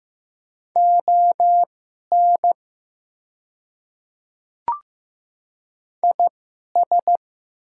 Morse code, heard as a keyed tone, about 15 wpm, spaced as for 11 wpm, 700 Hz. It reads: ON IS
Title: 2x Character Spacing - Top 100 Words - 15wpm